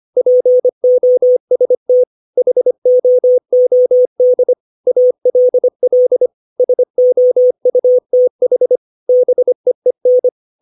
Morse code, heard as a keyed tone, about 25 wpm, 500 Hz. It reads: POST HOOD ALL SOUTH BEEN
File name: 632 words / 632 Words 25WPM